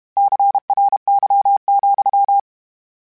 Morse code, heard as a keyed tone, about 32 wpm, 800 Hz. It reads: CRY,
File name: Misc / Misc-mp3s / Word Pairs @30